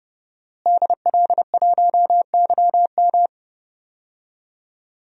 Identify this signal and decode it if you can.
Morse code, heard as a keyed tone, about 30 wpm, 700 Hz. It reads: DL1YM
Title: DXCC-100 Call Signs - 5 Characters - Repeated 2x - 30wpm